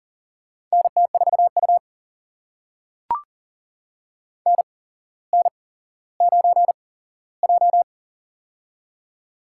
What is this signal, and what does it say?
Morse code, heard as a keyed tone, about 40 wpm, spaced as for 17 wpm, 700 Hz. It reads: NT4U NN9J
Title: USA Call Signs - 4 Characters - 8x Character Spacing - 40wpm